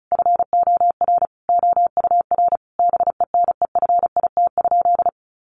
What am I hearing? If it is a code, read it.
Morse code, heard as a keyed tone, about 34 wpm, 700 Hz. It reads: FOR OUR BENEFIT?
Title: Sentences from Top 1500 Words 35wpm